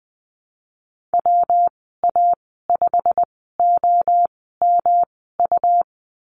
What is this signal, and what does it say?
Morse code, heard as a keyed tone, about 20 wpm, spaced as for 15 wpm, 700 Hz. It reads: WA5OMU
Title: 122 Lesson 20g - Number 3 - Call Signs 20wpm